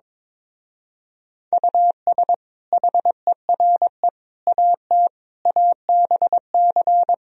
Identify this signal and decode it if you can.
Morse code, heard as a keyed tone, about 23 wpm, 700 Hz. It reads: US HERE AT ABC
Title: Speed-Racing - Sets of 5 Words 22wpm